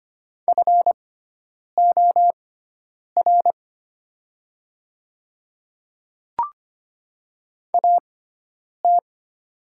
Morse code, heard as a keyed tone, about 25 wpm, spaced as for 9 wpm, 700 Hz. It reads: FOR AT
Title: 6x Character Spacing - Top 200 Words - 25wpm